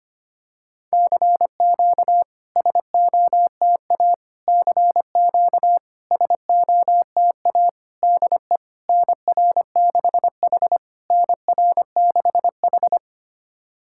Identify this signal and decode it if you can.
Morse code, heard as a keyed tone, about 25 wpm, 700 Hz. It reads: CQ SOTA CQ SOTA DE NR6H NR6H
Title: SOTA QSO Practice 25wpm